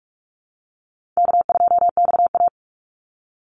Morse code, heard as a keyed tone, about 44 wpm, 700 Hz. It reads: K2XA
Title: Easy Call Signs - 45wpm